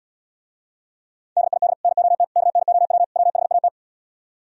Morse code, heard as a keyed tone, about 77 wpm, 700 Hz. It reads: DID NOT KNOW WHAT